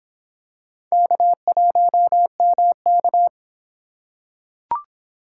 Morse code, heard as a keyed tone, about 24 wpm, 700 Hz. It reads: K1MK